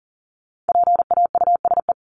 Morse code, heard as a keyed tone, about 40 wpm, 700 Hz. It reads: PAUSE
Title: Top 1501-1750 Words 40wpm